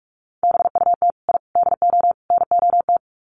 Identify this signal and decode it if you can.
Morse code, heard as a keyed tone, about 44 wpm, 700 Hz. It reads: BUT I DO NOT
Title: Sets of 4 Words 45wpm